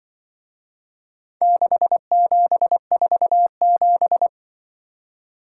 Morse code, heard as a keyed tone, about 24 wpm, 700 Hz. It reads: 6747
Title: Speed-Racing - Four Letter-Number Combinations - 20wpm